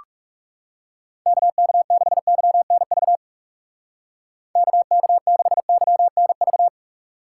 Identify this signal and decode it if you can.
Morse code, heard as a keyed tone, about 37 wpm, 700 Hz. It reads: KK6YNV KK6YNV
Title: Speed Racing - USA 2x3 Call Signs 30wpm